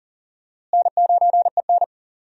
Morse code, heard as a keyed tone, about 40 wpm, 700 Hz. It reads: N9EN
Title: USA Call Signs - 4 Characters - Repeated 3x - 40wpm